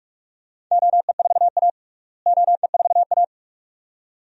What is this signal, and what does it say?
Morse code, heard as a keyed tone, about 46 wpm, 700 Hz. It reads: OE4A OE4A
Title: DXCC-100 Call Signs - 4 Characters - Repeated 2x - 45wpm